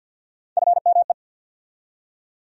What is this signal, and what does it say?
Morse code, heard as a keyed tone, about 48 wpm, 700 Hz. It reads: FME
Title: USA Call Sign Postfixes - 3 Letters 50wpm